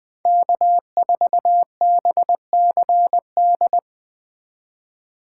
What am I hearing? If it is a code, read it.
Morse code, heard as a keyed tone, about 20 wpm, 700 Hz. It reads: K4BCD
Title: Speed Racing - USA Call Signs 17wpm